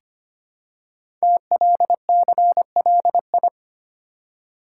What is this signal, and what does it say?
Morse code, heard as a keyed tone, about 25 wpm, 700 Hz. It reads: TLCLI
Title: Code Groups - 5 Letter:Numbers - 25wpm